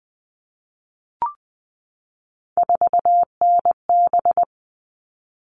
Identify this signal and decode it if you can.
Morse code, heard as a keyed tone, about 20 wpm, 700 Hz. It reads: 4NB
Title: All Cut Numbers - 3-Digit 20wpm